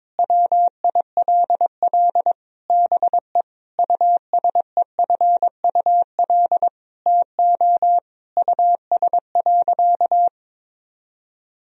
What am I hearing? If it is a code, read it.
Morse code, heard as a keyed tone, about 22 wpm, 700 Hz. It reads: WILL BE USEFUL TO US.
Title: Sentences from Top 1000 Words - Review Part 4 - 22wpm